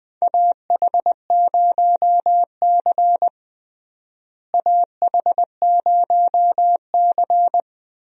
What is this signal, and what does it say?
Morse code, heard as a keyed tone, about 20 wpm, 700 Hz. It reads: AH0C AH0C
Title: USA Call Signs - 4 Characters - Repeated 2x - 20wpm